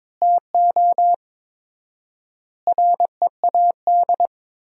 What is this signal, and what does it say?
Morse code, heard as a keyed tone, about 22 wpm, 700 Hz. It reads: TO READ
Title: 4x Word-Spacing - Sentences from Top 500 Words - 22wpm